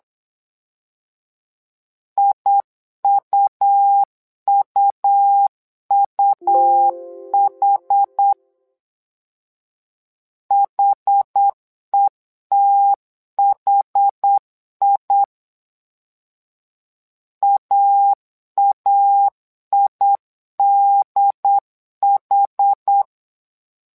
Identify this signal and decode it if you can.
Morse code, heard as a keyed tone, about 8 wpm, 800 Hz. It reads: IUUUH HETHI AAIDH
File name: LEZIONI / 2 UHDN / 2 UHD 8wpm 2258